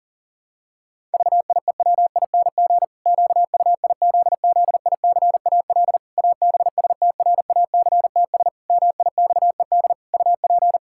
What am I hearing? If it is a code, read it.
Morse code, heard as a keyed tone, about 40 wpm, 700 Hz. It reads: VIEWING QUIZZICAL ABSTRACTS MIXED UP
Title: Pangram Challenge v2 - 40wpm